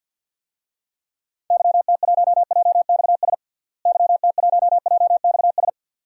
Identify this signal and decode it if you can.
Morse code, heard as a keyed tone, about 47 wpm, 700 Hz. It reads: YT1JXS YT1JXS
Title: DXCC-100 Call Signs - 6 Characters - Repeated 2x - 50wpm